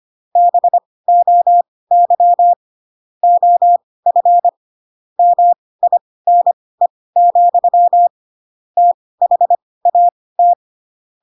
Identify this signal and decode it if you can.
Morse code, heard as a keyed tone, about 25 wpm, spaced as for 18 wpm, 700 Hz. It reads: BOY OF MINE, THAT
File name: Pinocchio / Pinocchio ch7 12 25 ik1hge